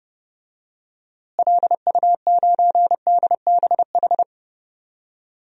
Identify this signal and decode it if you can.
Morse code, heard as a keyed tone, about 30 wpm, 700 Hz. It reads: LU9DBH
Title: DXCC-100 Call Signs - 6 Characters - Repeated 2x - 30wpm